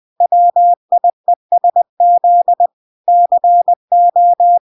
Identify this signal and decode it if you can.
Morse code, heard as a keyed tone, about 20 wpm, 700 Hz. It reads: WIESZ CO